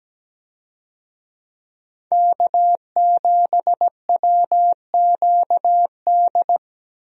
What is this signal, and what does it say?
Morse code, heard as a keyed tone, about 17 wpm, 700 Hz. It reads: K7WQD
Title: USA Call Signs - 5 Characters - 17wpm